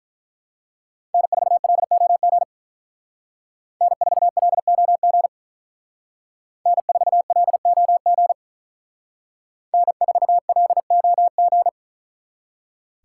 Morse code, heard as a keyed tone, about 42 wpm, 700 Hz. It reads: N4LOG N4LOG N4LOG N4LOG
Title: Speed Racing - USA 1x3 Call Signs 35wpm